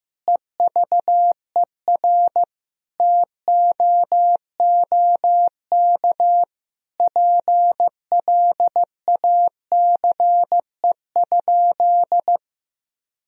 Morse code, heard as a keyed tone, about 15 wpm, 700 Hz. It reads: EVER TOOK PLACE?
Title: Sentences from Top 1000 Words - Review Part 1 - 15wpm